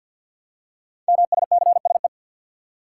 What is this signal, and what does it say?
Morse code, heard as a keyed tone, about 51 wpm, 700 Hz. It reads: MSCSE
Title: Code Groups - 5 Letter:Numbers - 50wpm